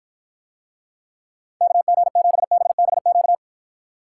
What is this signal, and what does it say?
Morse code, heard as a keyed tone, about 54 wpm, 700 Hz. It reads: KG7BBQ